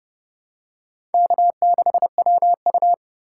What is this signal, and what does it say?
Morse code, heard as a keyed tone, about 27 wpm, 700 Hz. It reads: K6WU